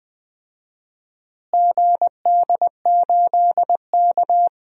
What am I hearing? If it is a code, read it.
Morse code, heard as a keyed tone, about 20 wpm, 700 Hz. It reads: GD8K
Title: DXCC-100 Call Signs - 4 Characters - Repeated 3x - 20wpm